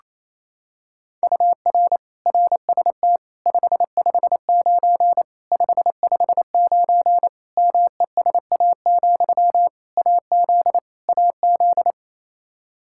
Morse code, heard as a keyed tone, about 28 wpm, 700 Hz. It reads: UR RST 559 559 MESA, AZ AZ
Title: Practice QSO - Exchange 1 - Beginning - 28wpm